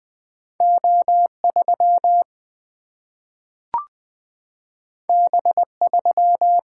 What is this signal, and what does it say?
Morse code, heard as a keyed tone, about 20 wpm, 700 Hz. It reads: O 3 B 3
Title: Single-Letter Single-Number 20wpm